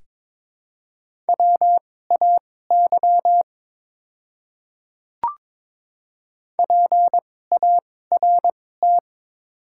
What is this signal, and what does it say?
Morse code, heard as a keyed tone, about 22 wpm, spaced as for 16 wpm, 700 Hz. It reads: WAY PART
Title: 2x Character Spacing - Top 200 Words - 22wpm